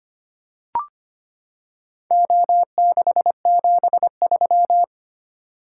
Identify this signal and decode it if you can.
Morse code, heard as a keyed tone, about 25 wpm, 700 Hz. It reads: O673
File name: practice / 25wpm / Four Letter-Number Combinations - 25wpm